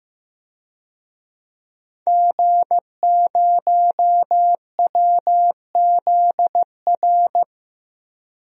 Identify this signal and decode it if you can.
Morse code, heard as a keyed tone, about 15 wpm, 700 Hz. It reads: G0WZR
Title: DXCC-100 Call Signs - 5 Characters - 15wpm